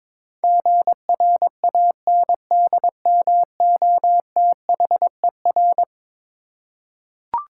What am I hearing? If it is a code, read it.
Morse code, heard as a keyed tone, about 22 wpm, 700 Hz. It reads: GRANDMOTHER
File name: practice / 22wpm / Eleven Letter Words - 22wpm